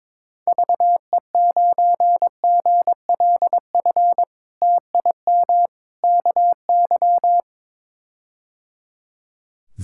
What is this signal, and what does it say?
Morse code, heard as a keyed tone, about 22 wpm, 700 Hz. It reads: VE9GLF TIM KY
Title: K1USN SST QSO Practice - 22wpm